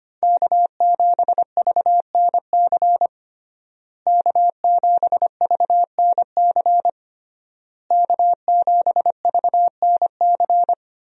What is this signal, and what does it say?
Morse code, heard as a keyed tone, about 25 wpm, 700 Hz. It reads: K7VNC K7VNC K7VNC
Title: USA Call Signs - 5 Characters - Repeated 3x - 25wpm